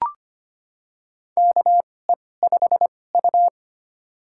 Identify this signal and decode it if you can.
Morse code, heard as a keyed tone, about 25 wpm, spaced as for 18 wpm, 700 Hz. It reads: KE5U
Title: USA Call Signs - 4 Characters - 2x Character Spacing - 25wpm